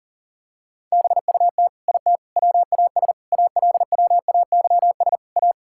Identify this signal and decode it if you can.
Morse code, heard as a keyed tone, about 40 wpm, 700 Hz. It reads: BUT IT WAS ALWAYS A
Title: Sentences from Top 1600 Words 40wpm